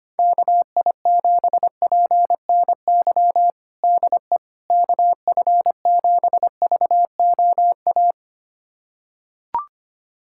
Morse code, heard as a keyed tone, about 25 wpm, 700 Hz. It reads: KI7PNY DE KF7VOA
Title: USA Callsign DE Callsign - 6 Characters - 25wpm